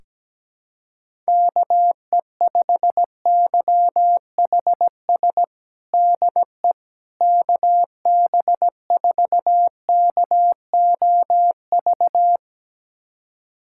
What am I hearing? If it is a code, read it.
Morse code, heard as a keyed tone, about 17 wpm, 700 Hz. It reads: KE5YHS DE KB4KOV
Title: USA Callsign DE Callsign - 6 Characters - 17wpm